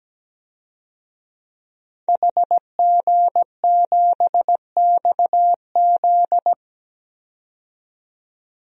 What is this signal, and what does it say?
Morse code, heard as a keyed tone, about 17 wpm, 700 Hz. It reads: HG7XZ
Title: DXCC-100 Call Signs - 5 Characters - 17wpm